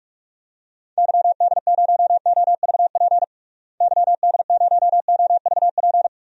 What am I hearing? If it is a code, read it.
Morse code, heard as a keyed tone, about 42 wpm, 700 Hz. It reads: YD0OVP YD0OVP
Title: DXCC-100 Call Signs - 6 Characters - Repeated 3x - 45wpm